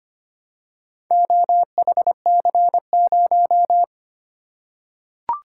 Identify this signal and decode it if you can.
Morse code, heard as a keyed tone, about 25 wpm, 700 Hz. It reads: OHC0
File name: practice / 25wpm / Four Letter-Number Combinations - 25wpm